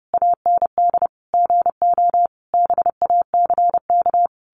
Morse code, heard as a keyed tone, about 30 wpm, 700 Hz. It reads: AND GO BACK
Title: A Sound Of Thunder 30wpm